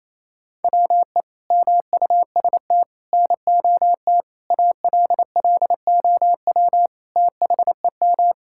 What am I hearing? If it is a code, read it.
Morse code, heard as a keyed tone, about 28 wpm, 700 Hz. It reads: WE MUST NOT ALLOW THEM
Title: Sentences from Top 1000 Words - Review Part 5 - 28wpm